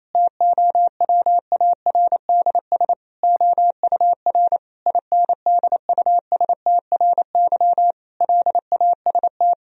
Morse code, heard as a keyed tone, about 27 wpm, 700 Hz. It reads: TOWARDS OUR INDUSTRY LAST
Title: Sentences from Top 1000 Words - Review Part 4 - 28wpm